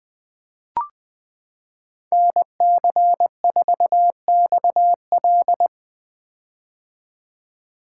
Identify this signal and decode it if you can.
Morse code, heard as a keyed tone, about 21 wpm, 700 Hz. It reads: NC4XL
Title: 2023 POTA Activator Call Signs - 20wpm